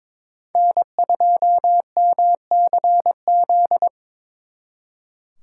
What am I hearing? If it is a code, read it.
Morse code, heard as a keyed tone, about 26 wpm, 700 Hz. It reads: N2MCZ
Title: Speed Racing - USA 1x3 Call Signs 22wpm